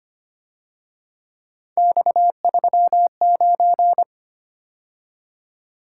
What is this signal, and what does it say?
Morse code, heard as a keyed tone, about 31 wpm, 700 Hz. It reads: X39
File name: Speed-Racing - Three Letter-Number Combinations - 25wpm